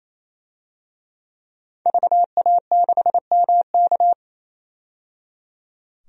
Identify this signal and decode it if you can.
Morse code, heard as a keyed tone, about 28 wpm, 700 Hz. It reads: VA6MK
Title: DXCC-100 Call Signs - 5 Characters - 28wpm